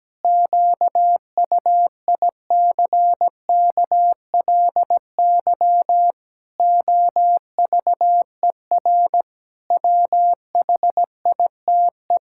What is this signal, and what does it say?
Morse code, heard as a keyed tone, about 17 wpm, 700 Hz. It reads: QUICKLY OVER WHITE
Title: Pangram Challenge v2 - 17wpm